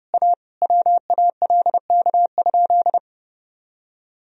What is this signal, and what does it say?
Morse code, heard as a keyed tone, about 31 wpm, 700 Hz. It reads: A WALK?
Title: Sentences from Top 1000 Words - Review Part 1 - 30wpm